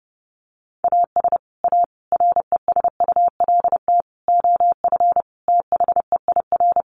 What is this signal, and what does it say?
Morse code, heard as a keyed tone, about 30 wpm, 700 Hz. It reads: AS A RESULT OF THEIR
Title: Sets of 5 Words Encore2 30wpm